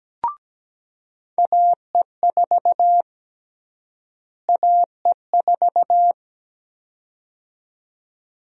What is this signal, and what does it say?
Morse code, heard as a keyed tone, about 17 wpm, 700 Hz. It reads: AE4 AE4
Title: USA Call Sign Prefixes - Repeated 2x - 17wpm